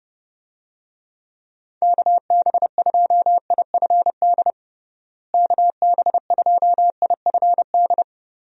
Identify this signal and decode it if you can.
Morse code, heard as a keyed tone, about 30 wpm, 700 Hz. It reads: KB2IFD KB2IFD
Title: USA Call Signs - 6 Characters - Repeated 2x - 30wpm